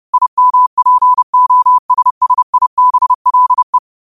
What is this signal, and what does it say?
Morse code, heard as a keyed tone, about 30 wpm, 1000 Hz. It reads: IMPOSSIBLE
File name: qotdfiles / 30wpm